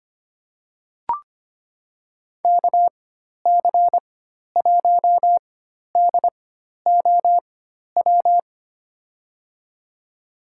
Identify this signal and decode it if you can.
Morse code, heard as a keyed tone, about 25 wpm, spaced as for 12 wpm, 700 Hz. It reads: KC1DOW
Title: USA Call Signs - 6 Characters - 4x Character Spacing - 25wpm